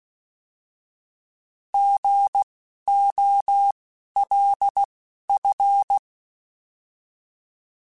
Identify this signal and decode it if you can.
Morse code, heard as a keyed tone, about 16 wpm, 800 Hz. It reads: G O L F
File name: Misc / Misc-mp3s / Misc @10